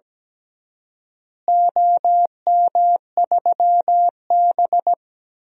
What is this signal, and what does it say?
Morse code, heard as a keyed tone, about 17 wpm, 700 Hz. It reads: OM3B
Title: DXCC-100 Call Signs - 4 Characters - Repeated 2x - 17wpm